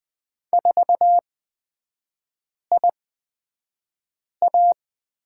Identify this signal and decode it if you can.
Morse code, heard as a keyed tone, about 20 wpm, spaced as for 5 wpm, 700 Hz. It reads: 4IA
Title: 255 Lesson 41e - Period - Three Characters 20wpm